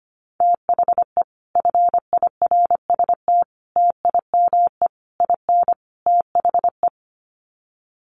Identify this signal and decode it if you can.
Morse code, heard as a keyed tone, about 25 wpm, 700 Hz. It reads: THE FIRST TIME IN THE
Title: Sets of 5 Words Encore 25wpm